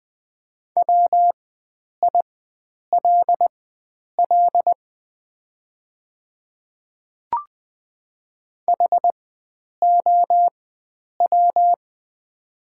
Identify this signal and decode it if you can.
Morse code, heard as a keyed tone, about 20 wpm, spaced as for 9 wpm, 690 Hz. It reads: WILL HOW
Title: 4x Character Spacing - Top 100 Words - 20wpm